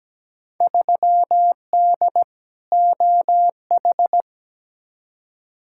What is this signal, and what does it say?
Morse code, heard as a keyed tone, about 17 wpm, 700 Hz. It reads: 3D OH
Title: Field Day Exchanges - 17wpm